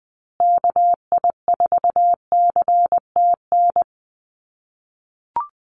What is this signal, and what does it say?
Morse code, heard as a keyed tone, about 20 wpm, 700 Hz. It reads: KI4CTN
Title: USA Call Signs Repeated 2x 20wpm